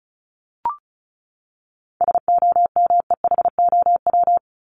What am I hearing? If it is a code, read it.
Morse code, heard as a keyed tone, about 34 wpm, 700 Hz. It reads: SOMEHOW
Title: Top 1501-1750 Words 35wpm